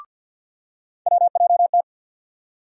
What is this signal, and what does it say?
Morse code, heard as a keyed tone, about 47 wpm, 700 Hz. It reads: WJT
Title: USA Call Sign Postfixes - 3 Letters 50wpm